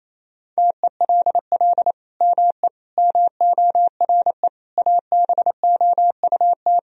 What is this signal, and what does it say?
Morse code, heard as a keyed tone, about 28 wpm, 700 Hz. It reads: TELL ME MORE ABOUT
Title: Sentences from Top 1000 Words - Review Part 3 - 28wpm